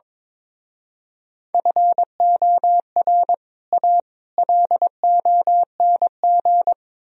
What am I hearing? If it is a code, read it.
Morse code, heard as a keyed tone, about 23 wpm, 700 Hz. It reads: FOR A LONG
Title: Speed-Racing - Sets of 3 Words 22wpm